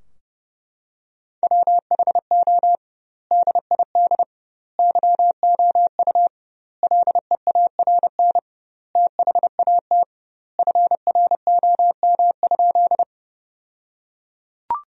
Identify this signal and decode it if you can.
Morse code, heard as a keyed tone, about 30 wpm, 700 Hz. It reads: WHO DID YOU LEARN THAT FROM?
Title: 2x Word-Spacing - Sentences from Top 300 Words - 30wpm